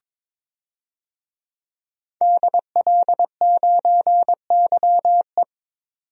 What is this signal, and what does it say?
Morse code, heard as a keyed tone, about 21 wpm, 700 Hz. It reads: DL9YE
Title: DXCC-100 Call Signs - 5 Characters - 22wpm